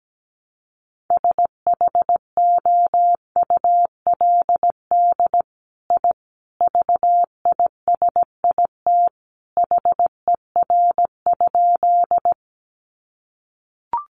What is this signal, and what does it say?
Morse code, heard as a keyed tone, about 17 wpm, 700 Hz. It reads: SHOULD I VISIT HER?
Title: Sentences from Top 900 Words 17wpm